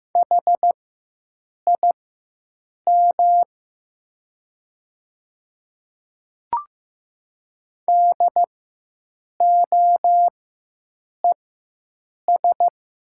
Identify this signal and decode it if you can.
Morse code, heard as a keyed tone, about 15 wpm, spaced as for 7 wpm, 700 Hz. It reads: HIM DOES
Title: 4x Character Spacing - Top 200 Words - 15wpm